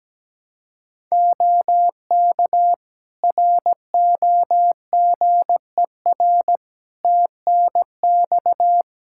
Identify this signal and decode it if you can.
Morse code, heard as a keyed tone, about 17 wpm, 700 Hz. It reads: OK ROGER TNX
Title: Practice QSO - Exchange 4 - Ending - 17wpm